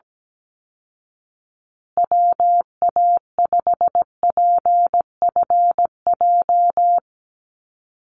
Morse code, heard as a keyed tone, about 17 wpm, 700 Hz. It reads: WA5PFJ
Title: Call Signs Repeated 2x - 17wpm